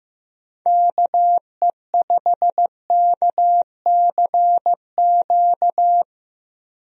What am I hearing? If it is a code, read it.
Morse code, heard as a keyed tone, about 15 wpm, 700 Hz. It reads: KE5KCQ
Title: USA Call Signs - 6 Characters - Repeated 2x - 15wpm